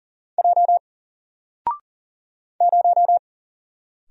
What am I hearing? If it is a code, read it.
Morse code, heard as a keyed tone, about 40 wpm, 700 Hz. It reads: J 0
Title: Single Letter-Number Alt 40wpm